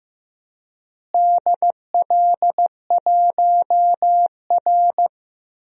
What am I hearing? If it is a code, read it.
Morse code, heard as a keyed tone, about 15 wpm, 700 Hz. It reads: DL1R
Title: DXCC-100 Call Signs - 4 Characters - Repeated 3x - 15wpm